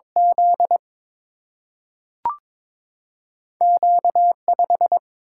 Morse code, heard as a keyed tone, about 22 wpm, 700 Hz. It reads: Z Q 5
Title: Single-Letter Single-Number 22wpm